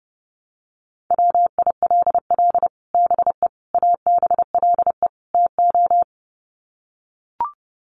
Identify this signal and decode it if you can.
Morse code, heard as a keyed tone, about 30 wpm, 700 Hz. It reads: WILL BE ABLE TO
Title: Sets of 4 Words 30wpm